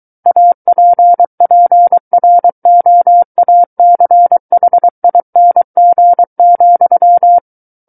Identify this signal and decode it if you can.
Morse code, heard as a keyed tone, about 23 wpm, 700 Hz. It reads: APPROACHING,
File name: Book One / 14-Chapter 14 - 23 WPM